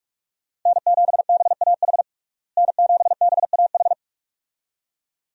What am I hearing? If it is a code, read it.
Morse code, heard as a keyed tone, about 43 wpm, 700 Hz. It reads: N7BAH N7BAH
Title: USA Call Signs - 5 Characters - Repeated 2x - 45wpm